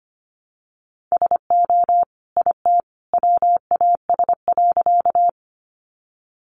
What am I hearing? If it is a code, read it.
Morse code, heard as a keyed tone, about 25 wpm, 700 Hz. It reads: SO IT WAS.
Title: Sentences from Top 100 Words 25wpm